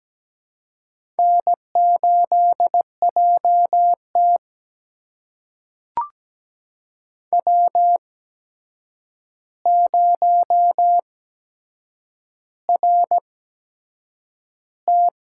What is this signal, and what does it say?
Morse code, heard as a keyed tone, about 17 wpm, spaced as for 7 wpm, 700 Hz. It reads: N8JT W0RT
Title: USA Call Signs - 4 Characters - 8x Character Spacing - 17wpm